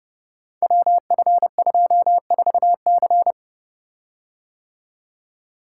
Morse code, heard as a keyed tone, about 30 wpm, 700 Hz. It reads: WF24C
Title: Code Groups - 5 Letter:Numbers - 30wpm